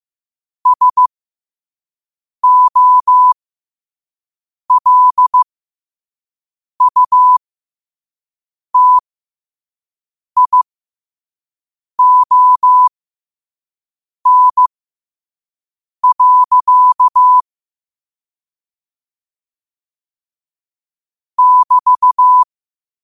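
Morse code, heard as a keyed tone, about 15 wpm, spaced as for 5 wpm, 1000 Hz. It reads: SOLUTION. <BT>
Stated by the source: Quote-of-the-day Morse code practice at 5 wpm.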